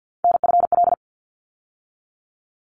Morse code, heard as a keyed tone, about 50 wpm, 700 Hz. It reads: NFL
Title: ARRL Sections 50wpm